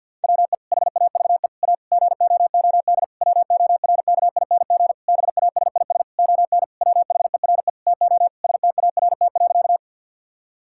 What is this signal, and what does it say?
Morse code, heard as a keyed tone, about 50 wpm, 700 Hz. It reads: WE HAVE A GOOD WORKING BASIS ON WHERE TO START.
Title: Sentences from Top 1000 Words - Review Part 5 - 50wpm